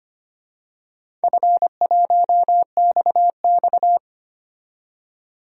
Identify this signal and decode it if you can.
Morse code, heard as a keyed tone, about 25 wpm, 700 Hz. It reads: F1XX
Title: DXCC-100 Call Signs - 4 Characters - 25wpm